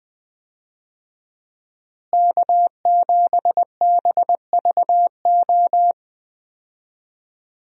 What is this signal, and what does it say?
Morse code, heard as a keyed tone, about 21 wpm, 700 Hz. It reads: K7BVO
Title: USA Call Signs - 5 Characters - Repeated 2x - 20wpm